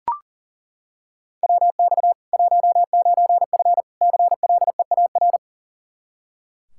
Morse code, heard as a keyed tone, about 40 wpm, 700 Hz. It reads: WX 19F CLEAR
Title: Practice QSO - Exchange 2 - Info - 40wpm